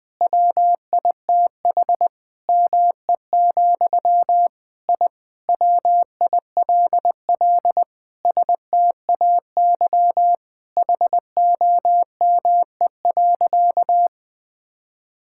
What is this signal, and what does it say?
Morse code, heard as a keyed tone, about 20 wpm, 700 Hz. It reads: WITH ME, I WILL STAY HOME.
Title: Sentences from Top 1000 Words - Review Part 6 - 20wpm